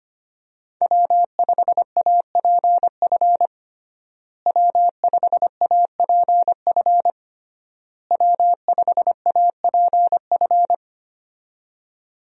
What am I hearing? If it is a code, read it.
Morse code, heard as a keyed tone, about 25 wpm, 700 Hz. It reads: W5APF W5APF W5APF
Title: USA Call Signs - 5 Characters - Repeated 3x - 25wpm